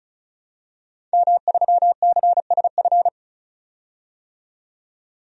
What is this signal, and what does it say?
Morse code, heard as a keyed tone, about 34 wpm, 700 Hz. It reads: M3CSF
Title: DXCC-100 Call Signs - 5 Characters - 35wpm